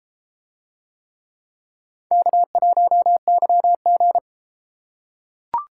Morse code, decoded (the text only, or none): K1YG